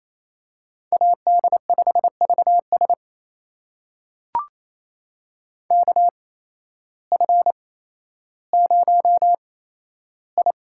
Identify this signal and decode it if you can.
Morse code, heard as a keyed tone, about 28 wpm, spaced as for 13 wpm, 700 Hz. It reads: AD5VS KF0I
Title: USA Call Signs - 5 Characters - 8x Character Spacing - 28wpm